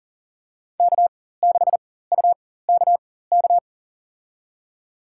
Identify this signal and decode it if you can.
Morse code, heard as a keyed tone, about 40 wpm, spaced as for 19 wpm, 700 Hz. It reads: K6UKK
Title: USA Call Signs - 5 Characters - 4x Character Spacing - 40wpm